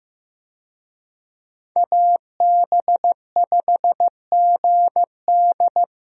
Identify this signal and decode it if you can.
Morse code, heard as a keyed tone, about 15 wpm, 700 Hz. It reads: AB5GD
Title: USA Call Signs - 5 Characters - Repeated 2x - 15wpm